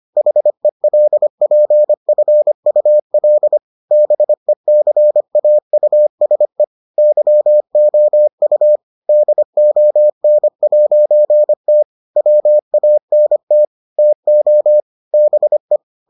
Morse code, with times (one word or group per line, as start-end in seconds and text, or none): 0.2-3.6 HELPFUL
3.9-6.6 BECAUSE
7.0-8.8 YOU
9.1-11.8 DON'T
12.2-13.6 WANT
14.0-14.8 TO
15.1-15.8 BE